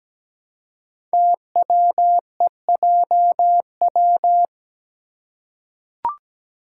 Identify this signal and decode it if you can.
Morse code, heard as a keyed tone, about 17 wpm, 700 Hz. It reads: TWEJW
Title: Code Groups - 5 Letters - 17wpm